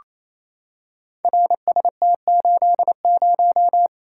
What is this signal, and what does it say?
Morse code, heard as a keyed tone, about 27 wpm, 700 Hz. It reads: RST80
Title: Code Groups - 5 Letter:Numbers - 28wpm